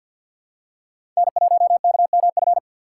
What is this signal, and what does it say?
Morse code, heard as a keyed tone, about 53 wpm, 700 Hz. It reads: N1KMF
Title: USA Call Signs - 5 Characters - 2x Character Spacing - 50wpm